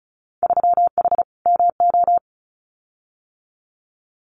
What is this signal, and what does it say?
Morse code, heard as a keyed tone, about 34 wpm, 700 Hz. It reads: 3H MO
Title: Winter Field Day Exchanges Encore 35wpm